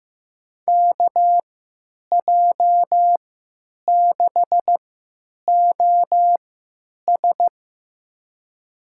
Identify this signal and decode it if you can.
Morse code, heard as a keyed tone, about 15 wpm, spaced as for 8 wpm, 700 Hz. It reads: KJ6OS
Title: USA Call Signs - 5 Characters - 3x Character Spacing - 15wpm